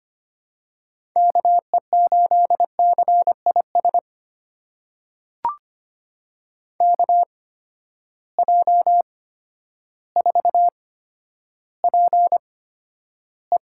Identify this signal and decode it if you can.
Morse code, heard as a keyed tone, about 25 wpm, spaced as for 11 wpm, 700 Hz. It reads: KE8CIS KJ4PE
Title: USA Call Signs - 8x Character Spacing - 25wpm